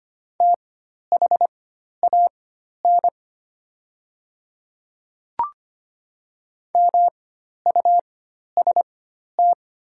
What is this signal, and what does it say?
Morse code, heard as a keyed tone, about 25 wpm, spaced as for 12 wpm, 700 Hz. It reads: THAN MUST
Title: 4x Character Spacing - Top 200 Words - 25wpm